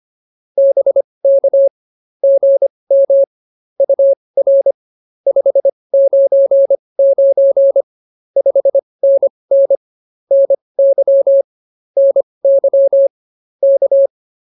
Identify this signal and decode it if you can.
Morse code, heard as a keyed tone, about 24 wpm, spaced as for 20 wpm, 550 Hz. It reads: BK GM UR 599 5NN NY NY K
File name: POTA210a 20